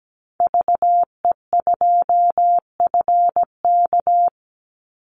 Morse code, heard as a keyed tone, about 17 wpm, 700 Hz. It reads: VE2FK
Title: Call Signs - 17wpm